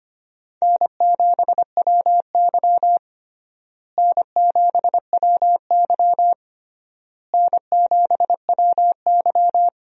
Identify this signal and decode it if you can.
Morse code, heard as a keyed tone, about 25 wpm, 700 Hz. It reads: N7WY N7WY N7WY
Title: USA Call Signs - 4 Characters - Repeated 3x - 25wpm